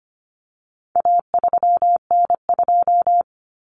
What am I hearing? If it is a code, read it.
Morse code, heard as a keyed tone, about 25 wpm, 700 Hz. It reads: A3N2
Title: Common Cut Numbers - 4-Digit 25wpm